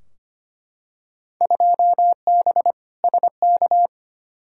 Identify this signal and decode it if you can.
Morse code, heard as a keyed tone, about 25 wpm, 700 Hz. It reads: 2B SK
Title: Field Day Exchanges Encore - 25wpm